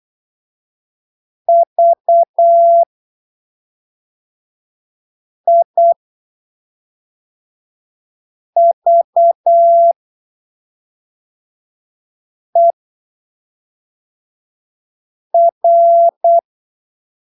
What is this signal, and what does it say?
Morse code, heard as a keyed tone, about 8 wpm, spaced as for 3 wpm, 680 Hz. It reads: VIVER
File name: arquivos / texto / texto 15 08ppm